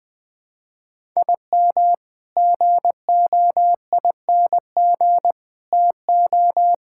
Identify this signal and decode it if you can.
Morse code, heard as a keyed tone, about 20 wpm, 700 Hz. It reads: IM GOING TO
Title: Sentences with 5-Word Phrases - 20wpm